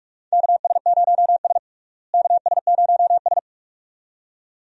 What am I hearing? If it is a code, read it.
Morse code, heard as a keyed tone, about 46 wpm, 700 Hz. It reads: KS0S KS0S
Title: USA Call Signs - 4 Characters - Repeated 2x - 45wpm